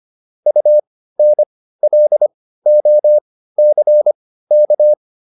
Morse code, heard as a keyed tone, about 25 wpm, spaced as for 15 wpm, 600 Hz. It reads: UNLOCK